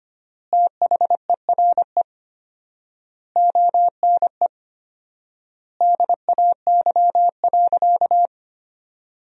Morse code, heard as a keyed tone, about 25 wpm, 700 Hz. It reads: THERE ONE DAY.
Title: 4x Word-Spacing - Sentences from Top 300 Words - 25wpm